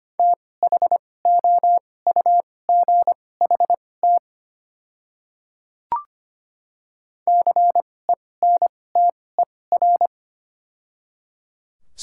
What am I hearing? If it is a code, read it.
Morse code, heard as a keyed tone, about 25 wpm, spaced as for 18 wpm, 700 Hz. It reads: THOUGHT CENTER
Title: 2x Character Spacing - Top 500 Words - 25wpm